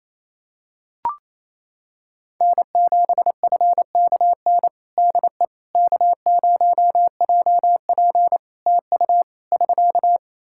Morse code, heard as a keyed tone, about 27 wpm, 700 Hz. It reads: N7FKN DE K0JP TU <SK>
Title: Practice QSO - Exchange 4 - Ending - 28wpm